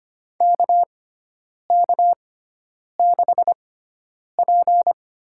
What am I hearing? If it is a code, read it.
Morse code, heard as a keyed tone, about 25 wpm, spaced as for 9 wpm, 700 Hz. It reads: KK6P
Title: USA Call Signs - 4 Characters - 6x Character Spacing - 25wpm